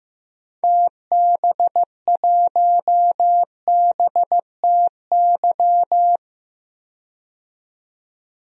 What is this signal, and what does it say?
Morse code, heard as a keyed tone, about 15 wpm, 700 Hz. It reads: TB1BTY
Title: DXCC-100 Call Signs - 6 Characters - 15wpm